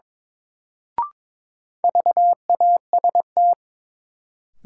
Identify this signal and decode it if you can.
Morse code, heard as a keyed tone, about 23 wpm, 700 Hz. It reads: VAST
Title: Sentences from Top 1300 Words - New Word Drill - 22wpm